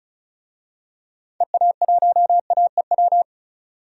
Morse code, heard as a keyed tone, about 36 wpm, 700 Hz. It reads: EA1AEW
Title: DXCC-100 Call Signs - 6 Characters - Repeated 3x - 35wpm